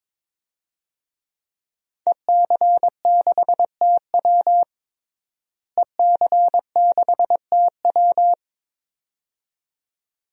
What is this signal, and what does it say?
Morse code, heard as a keyed tone, about 22 wpm, 700 Hz. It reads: EC6TW EC6TW
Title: DXCC-100 Call Signs - 5 Characters - Repeated 2x - 22wpm